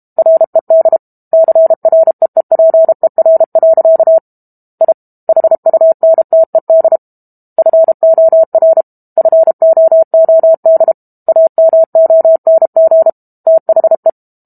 Morse code, heard as a keyed tone, about 32 wpm, 650 Hz. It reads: RED CREEPER. I HUNTED FOR FOOD AMONG THE
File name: Book Two / 06-Chapter 6 - 32 WPM